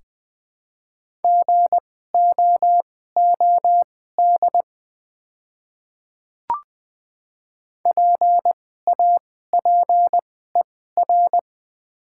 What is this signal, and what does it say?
Morse code, heard as a keyed tone, about 20 wpm, spaced as for 14 wpm, 700 Hz. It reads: GOOD PAPER
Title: 2x Character Spacing - Top 300 Words - 20wpm